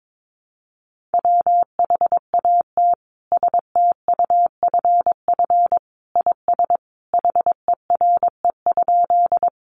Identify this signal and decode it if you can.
Morse code, heard as a keyed tone, about 22 wpm, 700 Hz. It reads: WHAT STUFF IS HERE?
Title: Sentences from Top 1400 Words 22wpm